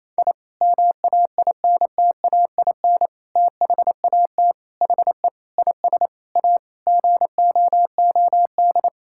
Morse code, heard as a keyed tone, about 28 wpm, 700 Hz. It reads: I MAINTAIN THAT HE IS A GOOD
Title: Sentences from Top 1000 Words - Review Part 5 - 28wpm